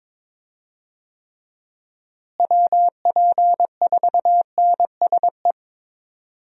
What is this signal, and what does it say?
Morse code, heard as a keyed tone, about 22 wpm, 700 Hz. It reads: WP4NSE
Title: USA Call Signs - 6 Characters - Repeated 2x - 22wpm